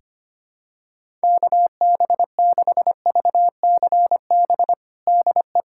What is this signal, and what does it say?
Morse code, heard as a keyed tone, about 25 wpm, 700 Hz. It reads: KB6VCB DE
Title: USA Callsign DE Callsign 25wpm